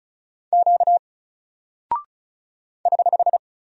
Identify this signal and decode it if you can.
Morse code, heard as a keyed tone, about 36 wpm, 700 Hz. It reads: Q <HH>
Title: Single Character (Letter:Number:Punctuation:Prosign) Alt 35wpm